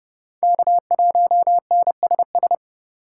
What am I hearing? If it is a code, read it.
Morse code, heard as a keyed tone, about 27 wpm, 700 Hz. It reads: K1NSS